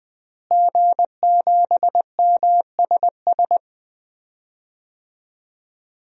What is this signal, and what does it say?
Morse code, heard as a keyed tone, about 20 wpm, 700 Hz. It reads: G7MSS
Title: DXCC-100 Call Signs - 5 Characters - Repeated 2x - 20wpm